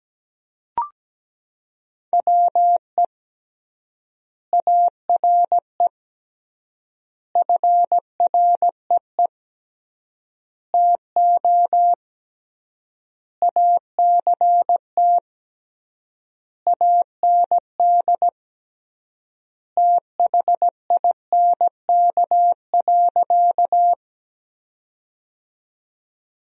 Morse code, heard as a keyed tone, about 17 wpm, 700 Hz. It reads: WE ARE FREE TO ACT AND THINK.
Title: 3x Word-Spacing - Sentences from Top 500 Words - 17wpm